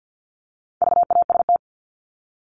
Morse code, heard as a keyed tone, about 50 wpm, 700 Hz. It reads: VAST
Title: Top 1251-1500 Words 50wpm